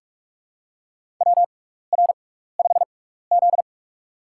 Morse code, heard as a keyed tone, about 44 wpm, spaced as for 15 wpm, 700 Hz. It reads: WR5Z
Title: USA Call Signs - 4 Characters - 6x Character Spacing - 45wpm